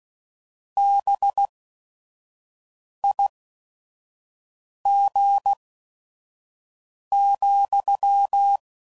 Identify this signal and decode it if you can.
Morse code, heard as a keyed tone, about 16 wpm, spaced as for 5 wpm, 800 Hz. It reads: BIG,